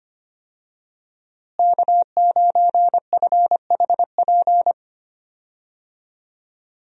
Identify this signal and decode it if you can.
Morse code, heard as a keyed tone, about 25 wpm, 700 Hz. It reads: K9FHP
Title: USA Call Signs - 5 Characters - Repeated 3x - 25wpm